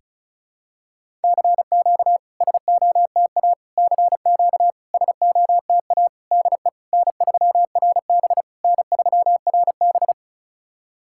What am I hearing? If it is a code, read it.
Morse code, heard as a keyed tone, about 36 wpm, 700 Hz. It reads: CQ SOTA CQ SOTA DE N3RB N3RB
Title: SOTA QSO Practice 35wpm